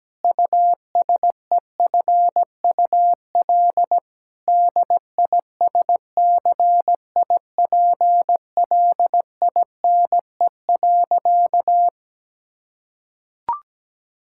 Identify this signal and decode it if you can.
Morse code, heard as a keyed tone, about 17 wpm, 700 Hz. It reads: USEFUL DISCIPLINE.
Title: Sentences from Top 1000 Words - Review Part 6 - 17wpm